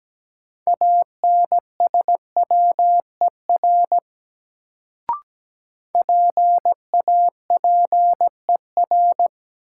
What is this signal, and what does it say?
Morse code, heard as a keyed tone, about 17 wpm, 700 Hz. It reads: ANSWER PAPER
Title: Sentences from Top 300 Words - New Word Drill - 17wpm